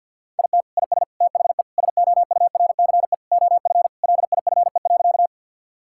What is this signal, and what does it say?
Morse code, heard as a keyed tone, about 51 wpm, 690 Hz. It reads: IT IS THE SOURCE OF LIFE.
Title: Sentences from Top 1000 Words - Review Part 5 - 50wpm